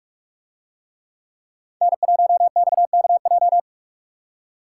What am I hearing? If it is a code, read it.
Morse code, heard as a keyed tone, about 44 wpm, 700 Hz. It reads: N1XKJ